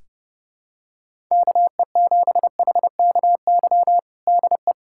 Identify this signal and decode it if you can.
Morse code, heard as a keyed tone, about 30 wpm, 700 Hz. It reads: KE7HKY DE
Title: USA Callsign DE Callsign - 6 Characters - 30wpm